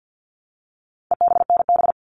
Morse code, heard as a keyed tone, about 50 wpm, 700 Hz. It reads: EBNB
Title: All Cut Numbers - 4-Digit 50wpm